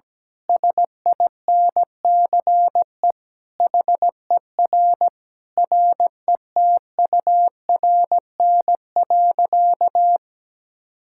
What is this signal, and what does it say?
Morse code, heard as a keyed tone, about 17 wpm, 700 Hz. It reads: SINCE HER RETURN.
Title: Sentences from Top 1000 Words - Review Part 6 - 17wpm